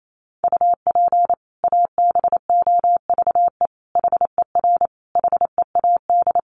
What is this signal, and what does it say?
Morse code, heard as a keyed tone, about 28 wpm, 700 Hz. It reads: UP ABOVE HER HEAD
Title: Sentences from Top 1200 Words 28wpm